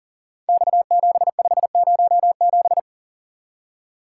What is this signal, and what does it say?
Morse code, heard as a keyed tone, about 40 wpm, 700 Hz. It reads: X7507
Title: Code Groups - 5 Letter:Numbers - 40wpm